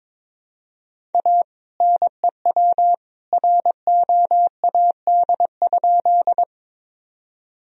Morse code, heard as a keyed tone, about 21 wpm, 700 Hz. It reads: A NEW ROAD?
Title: Sentences from Top 1000 Words - Review Part 2 - 22wpm